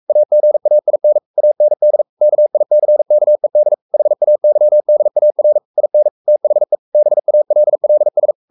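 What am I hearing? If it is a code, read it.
Morse code, heard as a keyed tone, about 44 wpm, 600 Hz. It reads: AGAIN AND KICKED HAYDAR IN THE BALLS
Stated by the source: Morse code at 43 wpm sending Ken Follett's novel Never in English, from CW med Gnister.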